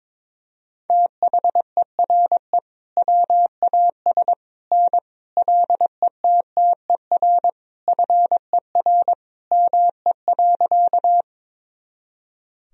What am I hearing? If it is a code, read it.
Morse code, heard as a keyed tone, about 21 wpm, 700 Hz. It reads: THERE WAS N LETTER FER ME.
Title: Sentences from Top 300 Words and Abbreviations 22wpm